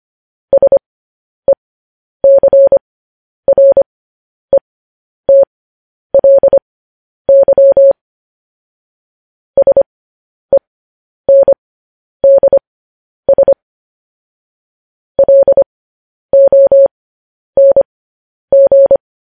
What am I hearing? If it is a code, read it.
Morse code, heard as a keyed tone, about 25 wpm, spaced as for 10 wpm, 550 Hz. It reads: SECRETLY SENDS LONG